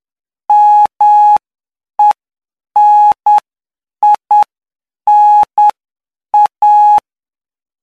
Morse code, heard as a keyed tone, about 9 wpm, spaced as for 7 wpm, 840 Hz. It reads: MENINA
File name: cw / py2lcd / 06-PY2LCD-Li��o 15